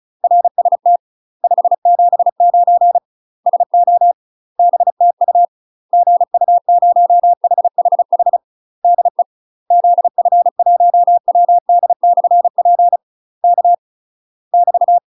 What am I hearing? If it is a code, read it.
Morse code, heard as a keyed tone, about 36 wpm, 700 Hz. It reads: RST 579 SO BTU GU0HHH DE ZF1WD/P K <BT>